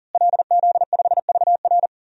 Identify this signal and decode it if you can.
Morse code, heard as a keyed tone, about 40 wpm, 700 Hz. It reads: LZ5VR
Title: DXCC-100 Call Signs - 5 Characters - Repeated 2x - 40wpm